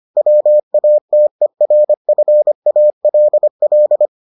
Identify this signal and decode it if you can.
Morse code, heard as a keyed tone, about 25 wpm, 600 Hz. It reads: WATERFALL